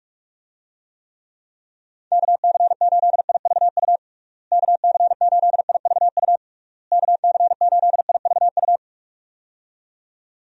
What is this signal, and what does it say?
Morse code, heard as a keyed tone, about 46 wpm, 700 Hz. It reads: KC8IVU KC8IVU KC8IVU
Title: USA Call Signs - 6 Characters - Repeated 3x - 45wpm